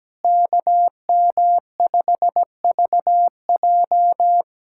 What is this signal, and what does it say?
Morse code, heard as a keyed tone, about 17 wpm, 700 Hz. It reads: KM5VJ
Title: USA Callsign DE Callsign - 5 Characters - 17wpm